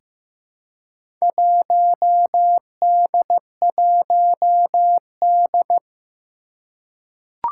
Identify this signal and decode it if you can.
Morse code, heard as a keyed tone, about 15 wpm, 700 Hz. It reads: 1D1D
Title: Four Challenging Letter-Number Combinations 15wpm